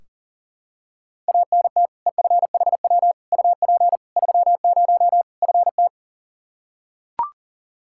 Morse code, heard as a keyed tone, about 40 wpm, 700 Hz. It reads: ANT EFHW UP 30 FT
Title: Practice QSO - Exchange 2 - Info - 40wpm